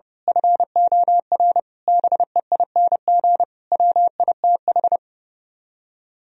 Morse code, heard as a keyed tone, about 28 wpm, 700 Hz. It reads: FOR BEING WITH
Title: Speed-Racing - Sets of 5 Words 30wpm